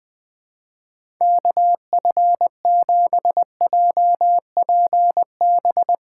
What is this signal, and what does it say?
Morse code, heard as a keyed tone, about 20 wpm, 700 Hz. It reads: KF7JPB
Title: USA Call Signs - 2x Character Spacing - 20wpm